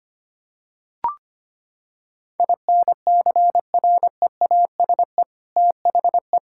Morse code, heard as a keyed tone, about 25 wpm, 700 Hz. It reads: INCREASE THE
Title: Sentences from Top 1000 Words - Review Part 6 - 25wpm